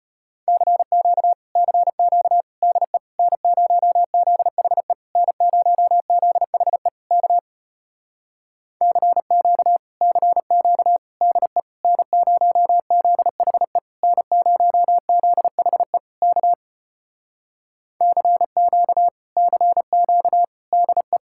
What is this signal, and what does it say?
Morse code, heard as a keyed tone, about 33 wpm, 700 Hz. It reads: CQ CQ DE N0ZHE N0ZHE K CQ CQ DE N0ZHE N0ZHE K CQ CQ DE